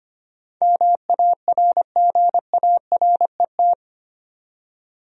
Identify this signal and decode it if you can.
Morse code, heard as a keyed tone, about 25 wpm, 700 Hz. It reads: MARGARET
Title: USA Common Names - 25wpm